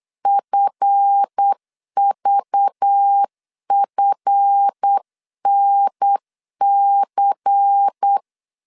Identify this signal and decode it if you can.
Morse code, heard as a keyed tone, about 8 wpm, 800 Hz. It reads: FVFNC
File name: LEZIONI / 7 GF9Z / 7 GF 8wpm 3108